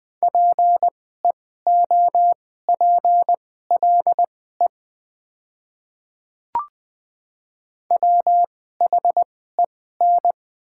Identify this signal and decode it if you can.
Morse code, heard as a keyed tone, about 20 wpm, spaced as for 14 wpm, 700 Hz. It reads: PEOPLE WHEN
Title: 2x Character Spacing - Top 100 Words - 20wpm